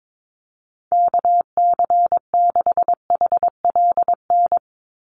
Morse code, heard as a keyed tone, about 22 wpm, 700 Hz. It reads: KC6HLN
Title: USA Call Signs 22wpm